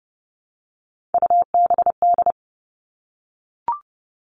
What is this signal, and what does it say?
Morse code, heard as a keyed tone, about 30 wpm, 700 Hz. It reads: UBD
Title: All Cut Numbers - 3-Digit 30wpm